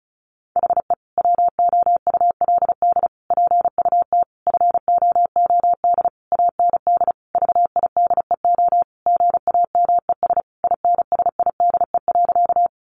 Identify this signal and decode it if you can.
Morse code, heard as a keyed tone, about 34 wpm, 700 Hz. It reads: HE WOULD PUT FOOD AND VIDEO GAMES INSIDE.
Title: Easy Phrases 35wpm